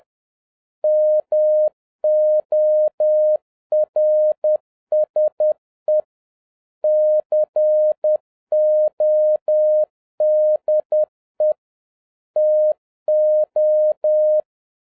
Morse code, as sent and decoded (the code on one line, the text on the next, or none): -- --- .-. ... / -.-. --- -.. / - ---
MORSE CODE TO